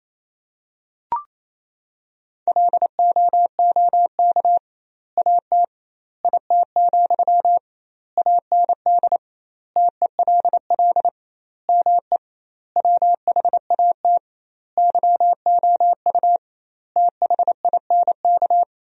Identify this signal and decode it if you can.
Morse code, heard as a keyed tone, about 27 wpm, 700 Hz. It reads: LOOK AT IT, AND TELL ME WHAT YOU THINK
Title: 2x Word-Spacing - Sentences from Top 200 Words - 28wpm